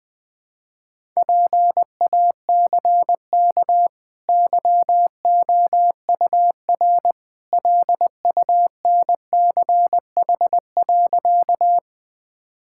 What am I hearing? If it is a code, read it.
Morse code, heard as a keyed tone, about 20 wpm, 700 Hz. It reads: PACK YOUR LUNCH.
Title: Easy Sentences with Hints - 20wpm